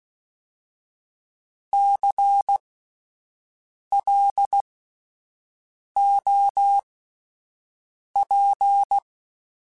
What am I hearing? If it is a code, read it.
Morse code, heard as a keyed tone, about 16 wpm, spaced as for 5 wpm, 800 Hz. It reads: CLOP
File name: Alphabetical / Alphabetical mp3s / Cs @5